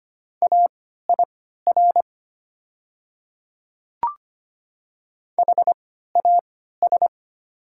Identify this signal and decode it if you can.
Morse code, heard as a keyed tone, about 25 wpm, spaced as for 14 wpm, 700 Hz. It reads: AIR HAS
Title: 3x Character Spacing - Top 300 Words - 25wpm